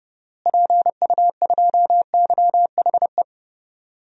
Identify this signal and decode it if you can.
Morse code, heard as a keyed tone, about 30 wpm, 700 Hz. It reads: PU2YHE
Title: DXCC-100 Call Signs - 6 Characters - Repeated 2x - 30wpm